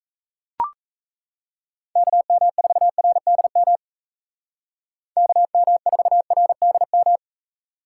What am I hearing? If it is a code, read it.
Morse code, heard as a keyed tone, about 33 wpm, 700 Hz. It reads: KM4RDM KM4RDM
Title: Speed Racing - USA 2x3 Call Signs 28wpm